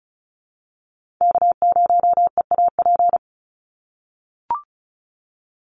Morse code, heard as a keyed tone, about 34 wpm, 700 Hz. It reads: K0EAP
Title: NA SOTA Chaser Callsigns 35wpm